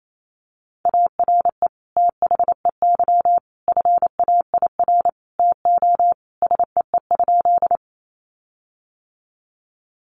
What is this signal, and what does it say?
Morse code, heard as a keyed tone, about 28 wpm, 700 Hz. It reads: ARE THEY FAIR TO SEE?
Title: Sentences from Top 1300 Words 28wpm